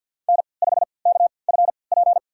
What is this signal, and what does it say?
Morse code, heard as a keyed tone, about 52 wpm, spaced as for 29 wpm, 700 Hz. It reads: N5KFP
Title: USA Call Signs - 5 Characters - 3x Character Spacing - 50wpm